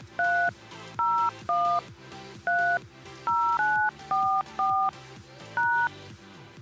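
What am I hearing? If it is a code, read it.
DTMF, heard as telephone touch tones, tones of about 320 ms. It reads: 30130944#
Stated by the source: Prank phone calls made on a show streamed live on YouTube.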